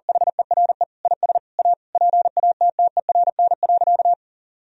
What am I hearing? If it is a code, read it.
Morse code, heard as a keyed tone, about 40 wpm, 700 Hz. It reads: HERE IS A PATTERN.
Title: Sentences from Top 1000 Words - Review Part 2 - 40wpm